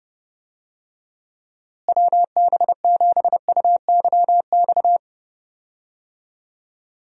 Morse code, heard as a keyed tone, about 31 wpm, 700 Hz. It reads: WB7UYX
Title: USA Call Signs - 6 Characters - Repeated 3x - 30wpm